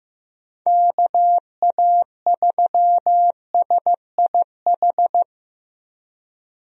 Morse code, heard as a keyed tone, about 15 wpm, 700 Hz. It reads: KA3SIH
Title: USA Call Signs - 6 Characters - Repeated 3x - 15wpm